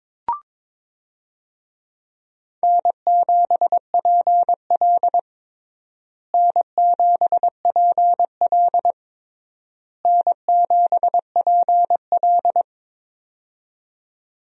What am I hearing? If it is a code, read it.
Morse code, heard as a keyed tone, about 21 wpm, 700 Hz. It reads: N7PL N7PL N7PL
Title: USA Call Signs - 4 Characters - Repeated 3x - 22wpm